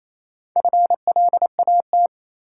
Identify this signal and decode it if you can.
Morse code, heard as a keyed tone, about 28 wpm, 700 Hz. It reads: FLAT
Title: Sentences from Top 1000 Words - New Word Drill - 28wpm